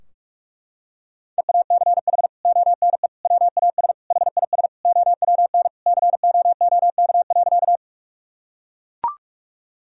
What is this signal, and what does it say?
Morse code, heard as a keyed tone, about 43 wpm, 700 Hz. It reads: EACH ONE WAS HIS OWN COOK.
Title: Sentences from Top 1000 Words - Review Part 4 - 45wpm